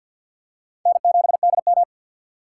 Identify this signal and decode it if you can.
Morse code, heard as a keyed tone, about 52 wpm, 700 Hz. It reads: N7DM
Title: USA Call Signs - 6x Character Spacing - 50wpm